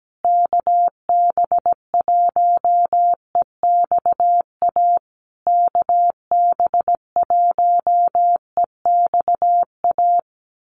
Morse code, heard as a keyed tone, about 17 wpm, 700 Hz. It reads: KB1EXA KB1EXA
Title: USA Call Signs Repeated 3x 17wpm